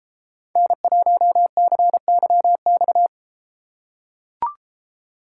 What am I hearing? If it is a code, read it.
Morse code, heard as a keyed tone, about 33 wpm, 700 Hz. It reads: N1CYX